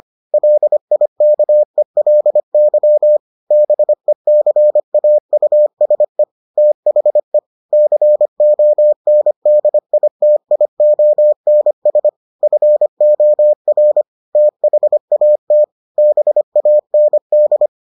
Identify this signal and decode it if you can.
Morse code, heard as a keyed tone, about 25 wpm, 600 Hz. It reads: LIKELY BECAUSE THE CONDITIONS FOR THAT BAND